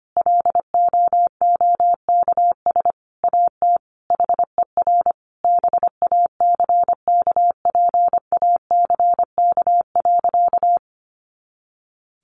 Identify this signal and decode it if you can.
Morse code, heard as a keyed tone, about 25 wpm, 700 Hz. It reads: LOOKS AT HER BACKPACK.
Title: Additional Easy Phrases 25wpm